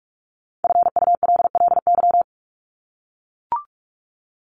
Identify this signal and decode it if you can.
Morse code, heard as a keyed tone, about 44 wpm, 700 Hz. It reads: FULLY
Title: Top 1001-1250 Words 45wpm